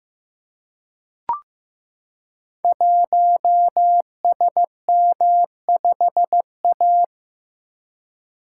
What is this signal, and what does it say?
Morse code, heard as a keyed tone, about 15 wpm, 700 Hz. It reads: 1SM5A
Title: Code Groups - 5 Letter:Numbers - 15wpm